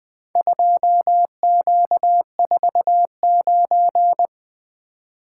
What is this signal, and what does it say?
Morse code, heard as a keyed tone, about 24 wpm, 700 Hz. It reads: 2Q49
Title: Speed-Racing - Four Letter-Number Combinations - 20wpm